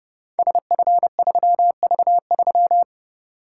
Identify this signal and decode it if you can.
Morse code, heard as a keyed tone, about 30 wpm, 700 Hz. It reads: SF3V3
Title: Code Groups - 5 Letter:Numbers - 30wpm